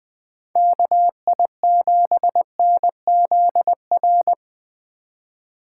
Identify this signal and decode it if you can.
Morse code, heard as a keyed tone, about 21 wpm, 700 Hz. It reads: KI7NZR
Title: Speed Racing - USA 2x3 Call Signs 17wpm